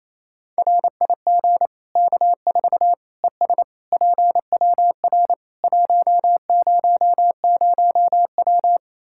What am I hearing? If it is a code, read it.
Morse code, heard as a keyed tone, about 28 wpm, 700 Hz. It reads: RIG K4 ES PWR 100W
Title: Practice QSO - Exchange 2 - Info - 28wpm